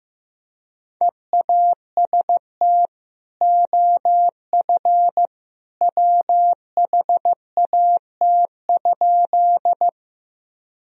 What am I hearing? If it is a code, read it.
Morse code, heard as a keyed tone, about 16 wpm, 700 Hz. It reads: EAST OF WHAT?
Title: Sentences from Top 1000 Words - Review Part 3 - 15wpm